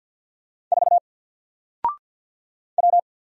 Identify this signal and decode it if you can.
Morse code, heard as a keyed tone, about 50 wpm, 700 Hz. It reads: 4 W
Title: Single Character (Letter:Number:Punctuation:Prosign) 50wpm